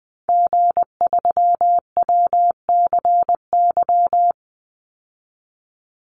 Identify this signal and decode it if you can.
Morse code, heard as a keyed tone, about 20 wpm, 700 Hz. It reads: G3WCY
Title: Call Signs - 20wpm